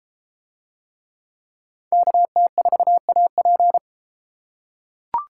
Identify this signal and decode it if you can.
Morse code, heard as a keyed tone, about 31 wpm, 700 Hz. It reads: KT4AP